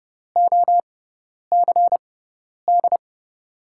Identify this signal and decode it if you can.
Morse code, heard as a keyed tone, about 30 wpm, spaced as for 10 wpm, 700 Hz. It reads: OCD
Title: USA Call Signs - 6x Character Spacing - 30wpm